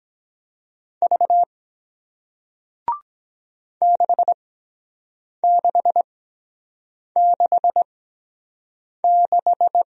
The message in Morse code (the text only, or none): V 6666